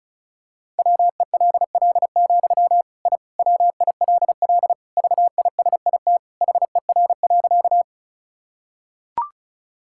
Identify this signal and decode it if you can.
Morse code, heard as a keyed tone, about 36 wpm, 700 Hz. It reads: WELL, I WILL VISIT HER.
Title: Sentences from Top 1000 Words - Review Part 6 - 35wpm